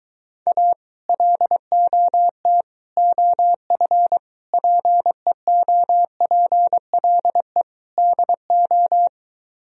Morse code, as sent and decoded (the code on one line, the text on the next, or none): .- / .-.. --- - / --- ..-. / .--. --- .--. .-.. / -.. ---
A LOT OF PEOPLE DO